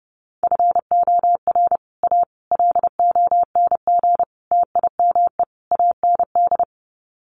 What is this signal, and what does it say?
Morse code, heard as a keyed tone, about 28 wpm, 700 Hz. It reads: FOR A LONG TIME AND
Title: Sets of 5 Words Encore2 30wpm